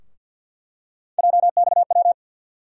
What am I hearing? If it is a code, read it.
Morse code, heard as a keyed tone, about 51 wpm, 700 Hz. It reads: JXW
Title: USA Call Sign Postfixes - 3 Letters 50wpm